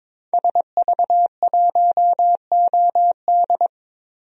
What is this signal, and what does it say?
Morse code, heard as a keyed tone, about 22 wpm, 700 Hz. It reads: SV1OD
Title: DXCC-100 Call Signs - 5 Characters - 22wpm